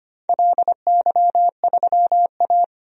Morse code, heard as a keyed tone, about 25 wpm, 700 Hz. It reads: LY3A
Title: DXCC-100 Call Signs - 4 Characters - 25wpm